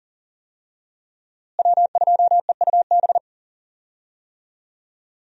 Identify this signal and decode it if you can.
Morse code, heard as a keyed tone, about 40 wpm, 700 Hz. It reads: W2EUB
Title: USA Call Signs - 5 Characters - Repeated 2x - 40wpm